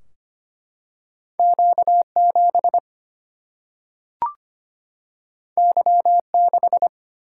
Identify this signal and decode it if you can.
Morse code, heard as a keyed tone, about 25 wpm, 700 Hz. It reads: Q7 Y6
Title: Single-Letter Single-Number 25wpm